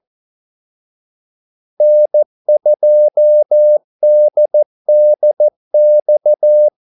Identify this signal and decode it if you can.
Morse code, heard as a keyed tone, about 14 wpm, 600 Hz. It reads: N2DDX
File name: Callsigns-14wpm